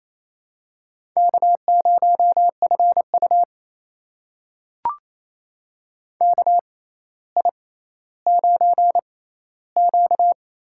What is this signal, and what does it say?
Morse code, heard as a keyed tone, about 28 wpm, spaced as for 14 wpm, 700 Hz. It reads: K0FU KI9Q
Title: USA Call Signs - 4 Characters - 6x Character Spacing - 28wpm